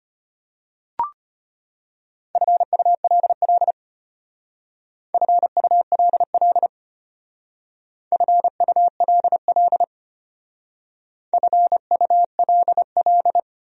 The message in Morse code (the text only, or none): FULL FULL FULL FULL